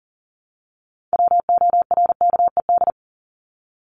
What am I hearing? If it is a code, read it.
Morse code, heard as a keyed tone, about 40 wpm, 700 Hz. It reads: WORKED
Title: ED Suffix - Top 100 Words with Base Words 40wpm